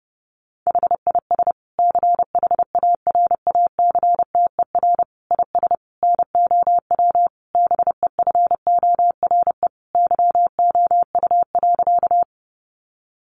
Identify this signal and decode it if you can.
Morse code, heard as a keyed tone, about 30 wpm, 700 Hz. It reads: HIS CHARACTER IS NOW BEFORE YOU.
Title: Sentences from Top 1200 Words 30wpm